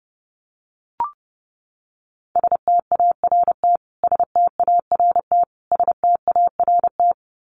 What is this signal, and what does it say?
Morse code, heard as a keyed tone, about 30 wpm, 700 Hz. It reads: START START START
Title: Top 300 Words Repeated 30wpm